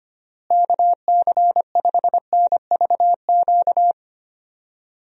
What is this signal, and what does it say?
Morse code, heard as a keyed tone, about 25 wpm, 700 Hz. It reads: KC5NVQ
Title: USA Call Signs - 6 Characters - Repeated 2x - 25wpm